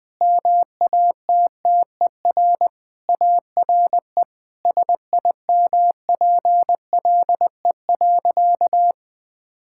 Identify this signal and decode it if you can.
Morse code, heard as a keyed tone, about 21 wpm, 700 Hz. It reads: MATTER ARE SIMPLE.
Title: Sentences with 3-Word Phrases - Spoken Beginning - 20wpm